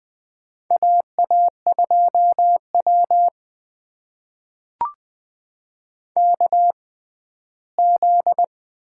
Morse code, heard as a keyed tone, about 20 wpm, spaced as for 14 wpm, 700 Hz. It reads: AA2W KZ
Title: USA Call Signs - 4 Characters - 6x Character Spacing - 20wpm